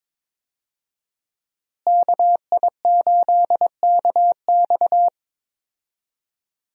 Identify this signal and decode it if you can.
Morse code, heard as a keyed tone, about 21 wpm, 700 Hz. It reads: KI8KX
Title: USA Call Signs - 5 Characters - Repeated 3x - 22wpm